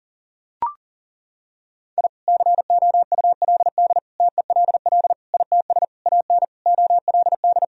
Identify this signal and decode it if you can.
Morse code, heard as a keyed tone, about 37 wpm, 700 Hz. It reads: I COULD TELL ITS AN OLD
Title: Sentences from Top 1000 Words - Review Part 2 - 40wpm